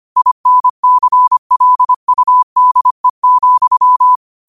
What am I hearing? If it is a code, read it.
Morse code, heard as a keyed tone, about 25 wpm, 1000 Hz. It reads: INCLUDE,
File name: qotdfiles / 25wpm